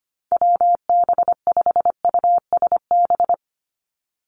Morse code, heard as a keyed tone, about 25 wpm, 700 Hz. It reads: WB5USB
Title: NA SOTA Chaser Callsigns 25wpm